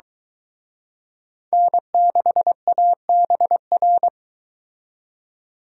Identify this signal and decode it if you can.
Morse code, heard as a keyed tone, about 23 wpm, 700 Hz. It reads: N6ABR